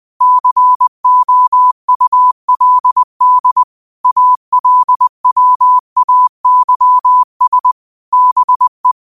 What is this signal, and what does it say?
Morse code, heard as a keyed tone, about 20 wpm, 1000 Hz. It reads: COULD ALWAYS BE
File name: qotdfiles / 20wpm